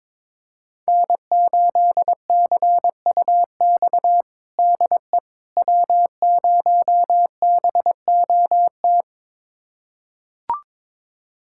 Morse code, heard as a keyed tone, about 22 wpm, 700 Hz. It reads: N8CUX DE W0BOT
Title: USA Callsign DE Callsign - 5 Characters - 22wpm